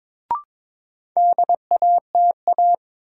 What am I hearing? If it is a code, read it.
Morse code, heard as a keyed tone, about 22 wpm, 700 Hz. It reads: DATA
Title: Sentences from Top 1000 Words - Review Part 3 - New Word Drill - 22wpm